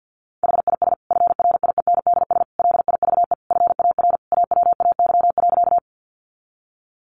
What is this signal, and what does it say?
Morse code, heard as a keyed tone, about 50 wpm, 700 Hz. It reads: HIS FRIENDS LIVE FAR AWAY.
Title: Additional Easy Phrases 50wpm